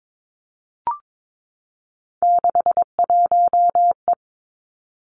Morse code, heard as a keyed tone, about 22 wpm, 700 Hz. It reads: 61E
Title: Three Challenging Letter-Number Combinations 22wpm